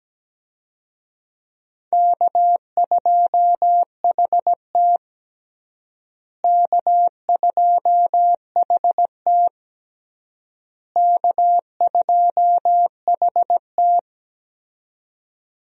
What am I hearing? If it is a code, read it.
Morse code, heard as a keyed tone, about 17 wpm, 700 Hz. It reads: K2HT K2HT K2HT
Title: USA Call Signs - 4 Characters - Repeated 3x - 17wpm